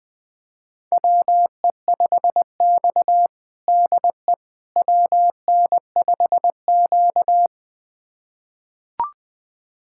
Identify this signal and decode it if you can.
Morse code, heard as a keyed tone, about 21 wpm, 700 Hz. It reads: WE5X DE WN5Q
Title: USA Callsign DE Callsign - 4 Characters - 20wpm